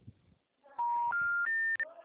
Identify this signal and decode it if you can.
Special information tone.